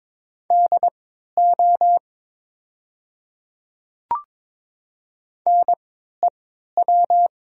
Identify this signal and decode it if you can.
Morse code, heard as a keyed tone, about 22 wpm, spaced as for 12 wpm, 700 Hz. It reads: DO NEW
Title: 3x Character Spacing - Top 300 Words - 22wpm